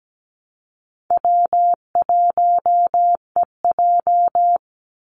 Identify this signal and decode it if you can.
Morse code, heard as a keyed tone, about 17 wpm, 700 Hz. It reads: W1EJ
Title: NA SOTA Activator Callsigns 17wpm